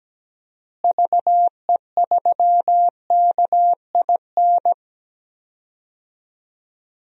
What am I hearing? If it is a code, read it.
Morse code, heard as a keyed tone, about 17 wpm, 700 Hz. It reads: VE3KIN
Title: DXCC-100 Call Signs - 6 Characters - Repeated 3x - 17wpm